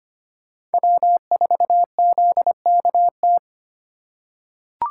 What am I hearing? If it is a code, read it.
Morse code, heard as a keyed tone, about 25 wpm, 700 Hz. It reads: W4ZKT
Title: USA Call Signs - 5 Characters - 2x Character Spacing - 25wpm